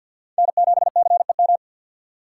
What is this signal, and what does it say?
Morse code, heard as a keyed tone, about 53 wpm, 700 Hz. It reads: N7CEM
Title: USA Call Signs - 5 Characters - 8x Character Spacing - 50wpm